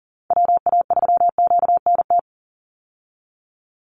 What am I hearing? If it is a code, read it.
Morse code, heard as a keyed tone, about 40 wpm, 700 Hz. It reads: WA3QNT
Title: Call Signs Repeated 2x - 40wpm